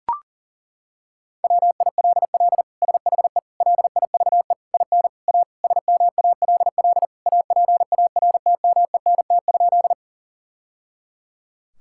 Morse code, heard as a keyed tone, about 40 wpm, 700 Hz. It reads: WILL SHE LIVE IN A SMALL APARTMENT?
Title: Sentences from Top 1000 Words - Review Part 6 - 40wpm